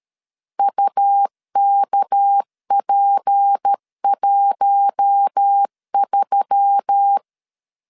Morse code, heard as a keyed tone, about 13 wpm, 800 Hz. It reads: UKP13
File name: LEZIONI / 6 PKX0 / 6 PK 12wpm 2006